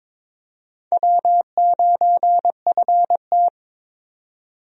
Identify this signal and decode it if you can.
Morse code, heard as a keyed tone, about 22 wpm, 700 Hz. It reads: W9FT
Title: USA Call Signs - 4 Characters - 22wpm